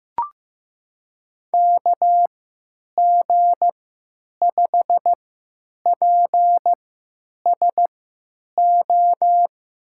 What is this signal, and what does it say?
Morse code, heard as a keyed tone, about 15 wpm, spaced as for 9 wpm, 700 Hz. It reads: KG5PSO
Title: USA Call Signs - 6 Characters - 3x Character Spacing - 15wpm